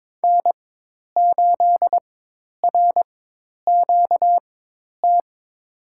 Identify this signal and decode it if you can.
Morse code, heard as a keyed tone, about 22 wpm, spaced as for 10 wpm, 700 Hz. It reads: N8RQT
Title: USA Call Signs - 5 Characters - 4x Character Spacing - 22wpm